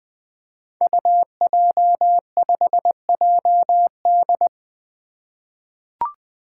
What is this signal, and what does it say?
Morse code, heard as a keyed tone, about 20 wpm, 700 Hz. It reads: UJ5JD
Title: Code Groups - 5 Letter:Numbers - 20wpm